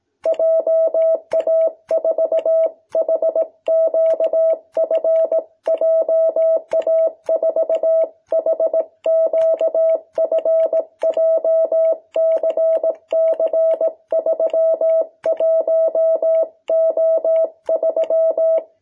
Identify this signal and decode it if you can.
Morse code, heard as a keyed tone, about 18 wpm, 650 Hz. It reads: JA4HQF JA4HQF JCC31O3
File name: jh8cbh / 2021cw